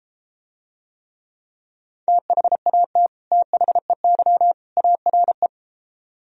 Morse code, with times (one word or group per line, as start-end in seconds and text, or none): 2.1-3.1 THAT
3.3-4.5 THEY
4.8-5.5 ARE